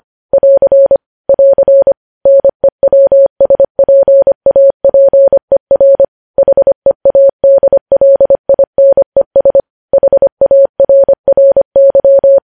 Morse code, heard as a keyed tone, about 25 wpm, 550 Hz. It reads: <AR> <AR> NEWSPAPER HEADLINES HARRY
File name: podcast / News headlines-1x-25x25